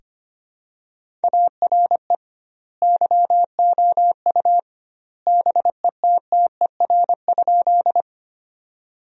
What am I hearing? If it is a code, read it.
Morse code, heard as a keyed tone, about 25 wpm, 700 Hz. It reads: ARE YOU BETTER?
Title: 2x Word-Spacing - Sentences from Top 400 Words - 25wpm